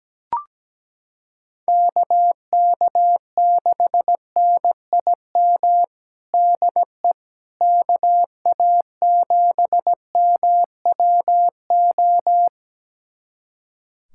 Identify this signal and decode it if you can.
Morse code, heard as a keyed tone, about 17 wpm, 700 Hz. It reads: KK6NIM DE KA7MWO
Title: USA Callsign DE Callsign - 6 Characters - 17wpm